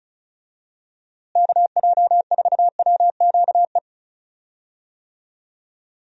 Morse code, heard as a keyed tone, about 36 wpm, 700 Hz. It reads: KJ4WQE
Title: USA Call Signs - 6 Characters - Repeated 2x - 35wpm